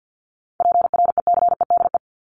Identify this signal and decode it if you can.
Morse code, heard as a keyed tone, about 48 wpm, 700 Hz. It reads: PRECEDE
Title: ING Suffix - Top 100 Words with Base Words 50wpm